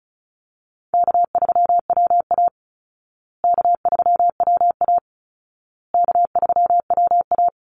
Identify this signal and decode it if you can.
Morse code, heard as a keyed tone, about 34 wpm, 700 Hz. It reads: K3WA K3WA K3WA
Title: Call Signs Repeated 2x - 35wpm